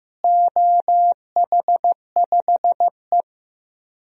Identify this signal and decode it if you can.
Morse code, heard as a keyed tone, about 15 wpm, 700 Hz. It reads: OH5E
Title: DXCC-100 Call Signs - 4 Characters - Repeated 2x - 15wpm